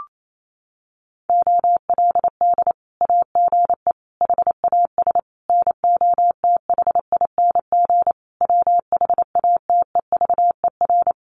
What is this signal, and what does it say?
Morse code, heard as a keyed tone, about 28 wpm, 700 Hz. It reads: OLD AGE HAS NOTHING WHATEVER
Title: Sentences from Top 1100 Words 28wpm